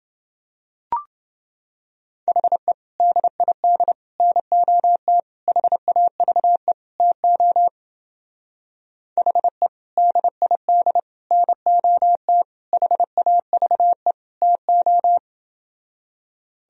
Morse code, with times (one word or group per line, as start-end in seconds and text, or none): 2.3-2.7 HE
3.0-3.9 DID
4.2-5.2 NOT
5.5-6.7 HAVE
7.0-7.7 TO
9.2-9.7 HE
10.0-11.0 DID
11.3-12.4 NOT
12.7-14.1 HAVE
14.4-15.2 TO